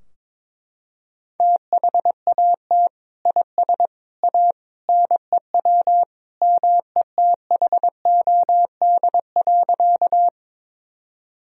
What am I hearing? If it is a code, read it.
Morse code, heard as a keyed tone, about 22 wpm, 700 Hz. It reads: THAT IS A NEW METHOD.
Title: Sentences from Top 1000 Words - Review Part 3 - 22wpm